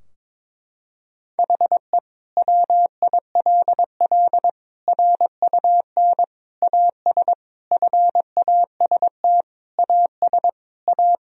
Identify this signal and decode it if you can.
Morse code, heard as a keyed tone, about 22 wpm, 700 Hz. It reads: HE WILL RUN AS FAST AS A
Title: Sentences from Top 1000 Words - Review Part 2 - 22wpm